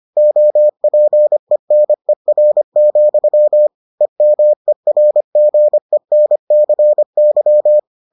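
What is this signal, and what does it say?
Morse code, heard as a keyed tone, about 25 wpm, 600 Hz. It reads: OPENER, EMERGENCY
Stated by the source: Morse edition of the Foundations of Amateur Radio episodes.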